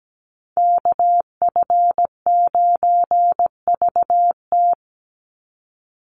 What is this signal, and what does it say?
Morse code, heard as a keyed tone, about 17 wpm, 700 Hz. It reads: KF9VT
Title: USA Call Signs 17wpm